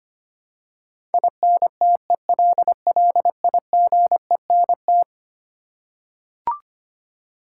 Morse code, heard as a keyed tone, about 25 wpm, 700 Hz. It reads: INTELLIGENT
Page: morse-code-ninja-practice-25wpm